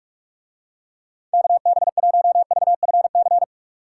Morse code, heard as a keyed tone, about 44 wpm, 700 Hz. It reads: KB1VFC